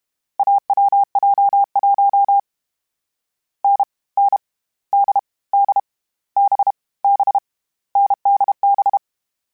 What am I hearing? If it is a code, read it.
Morse code, heard as a keyed tone, about 32 wpm, 800 Hz. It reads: AWJ1 N N D D B B NDB
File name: Difficult Combos / Difficult Combos-mp3s / awj1 ndb6 gz7 @30